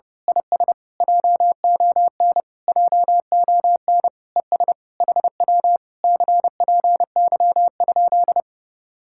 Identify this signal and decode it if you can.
Morse code, heard as a keyed tone, about 30 wpm, 700 Hz. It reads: IS JON JON ES HW CPY?
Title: Speed-Racing Common QSO Phrases Encore - 30wpm